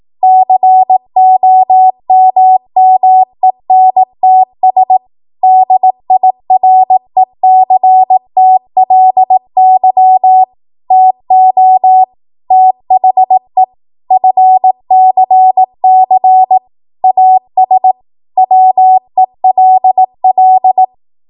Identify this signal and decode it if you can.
Morse code, heard as a keyed tone, about 18 wpm, 750 Hz. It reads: COMMENTS DIRECTLY TO THE FCC AS WELL
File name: Morse / 18WPM